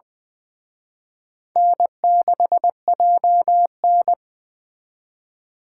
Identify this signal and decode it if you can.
Morse code, heard as a keyed tone, about 23 wpm, 700 Hz. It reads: N6JN
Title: Speed Racing - USA Call Signs 20wpm